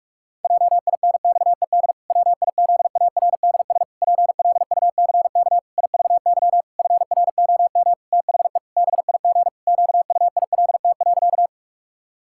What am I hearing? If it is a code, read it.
Morse code, heard as a keyed tone, about 46 wpm, 700 Hz. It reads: JINXED WIZARDS PLUCK IVY FROM THE BIG QUILT.
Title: Pangram Challenge v2 - 45wpm